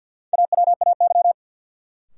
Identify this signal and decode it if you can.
Morse code, heard as a keyed tone, about 51 wpm, 700 Hz. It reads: AWAY
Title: Sets of 2 Words - New Word Drill - 50wpm